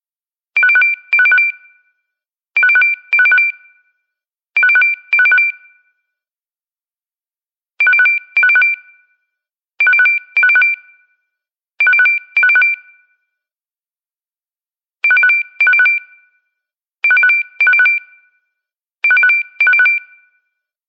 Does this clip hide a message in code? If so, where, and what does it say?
No.